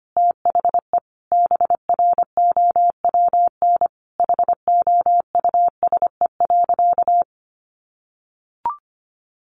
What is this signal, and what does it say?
Morse code, heard as a keyed tone, about 25 wpm, 700 Hz. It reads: THE BROWN HOUSE.
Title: Sentences from Top 1200 Words 25wpm